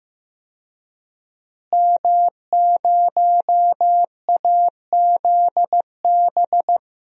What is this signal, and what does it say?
Morse code, heard as a keyed tone, about 15 wpm, 700 Hz. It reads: M0AZB
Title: DXCC-100 Call Signs - 5 Characters - Repeated 2x - 15wpm